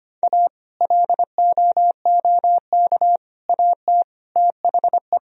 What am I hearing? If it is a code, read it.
Morse code, heard as a keyed tone, about 33 wpm, 700 Hz. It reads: A LOOK AT THE